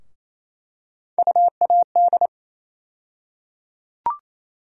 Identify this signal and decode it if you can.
Morse code, heard as a keyed tone, about 28 wpm, 700 Hz. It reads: UAD
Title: USA Call Sign Postfixes - 3 Letters 28wpm